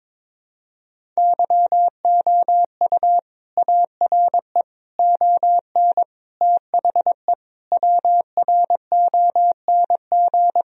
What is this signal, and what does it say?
Morse code, heard as a keyed tone, about 22 wpm, 700 Hz. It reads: YOU ARE ON THE WRONG
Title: Sentences from Top 1000 Words - Review Part 1 - 22wpm